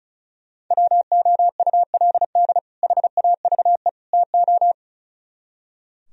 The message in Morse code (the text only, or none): WOULD HAVE TO